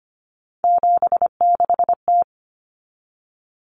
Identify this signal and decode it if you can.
Morse code, heard as a keyed tone, about 25 wpm, 700 Hz. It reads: 76T
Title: Common Cut Numbers - 3-Digit 25wpm